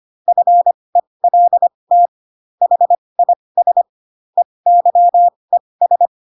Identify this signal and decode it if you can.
Morse code, heard as a keyed tone, about 25 wpm, spaced as for 20 wpm, 700 Hz. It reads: FELT HIS EYES
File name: Pinocchio / Pinocchio ch8 15 25 ik1hge